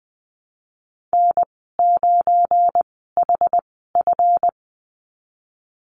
Morse code, heard as a keyed tone, about 19 wpm, spaced as for 14 wpm, 700 Hz. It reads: N9HF
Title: 186 Lesson 30g - Letter b - Call Signs 20wpm